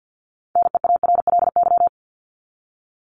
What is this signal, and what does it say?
Morse code, heard as a keyed tone, about 50 wpm, 700 Hz. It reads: NEARLY
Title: Top 1001-1250 Words 50wpm